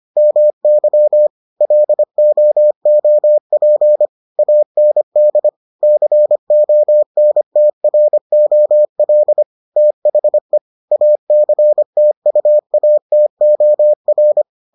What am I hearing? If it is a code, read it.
Morse code, heard as a keyed tone, about 25 wpm, 600 Hz. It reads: MY LOOP AND CONTROL THE ACTUATOR